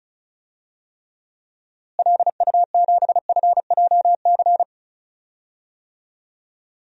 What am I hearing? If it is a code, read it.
Morse code, heard as a keyed tone, about 36 wpm, 700 Hz. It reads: LU7FJC
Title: DXCC-100 Call Signs - 6 Characters - Repeated 3x - 35wpm